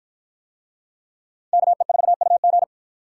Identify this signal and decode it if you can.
Morse code, heard as a keyed tone, about 53 wpm, 700 Hz. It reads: KE4UG